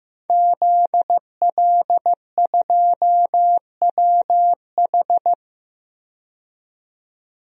Morse code, heard as a keyed tone, about 15 wpm, 700 Hz. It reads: ZL2WH
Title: DXCC-100 Call Signs - 5 Characters - 15wpm